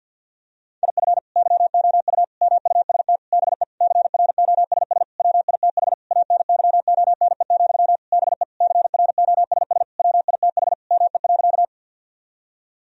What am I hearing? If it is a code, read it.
Morse code, heard as a keyed tone, about 52 wpm, 700 Hz. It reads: IF YOU MUST BE CROSS WITH ANYONE, BE CROSS WITH ME.
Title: Sentences from Top 1000 Words - Review Part 1 - 50wpm